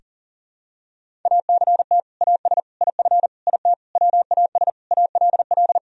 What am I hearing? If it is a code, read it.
Morse code, heard as a keyed tone, about 40 wpm, 700 Hz. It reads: ACT AS IF IT WAS ALL
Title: Sentences from Top 1000 Words - Review Part 1 - 40wpm